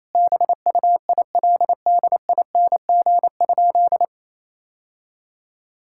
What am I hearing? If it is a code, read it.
Morse code, heard as a keyed tone, about 28 wpm, 700 Hz. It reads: BUILDING?
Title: Sentences from Top 1000 Words - Review Part 5 - 28wpm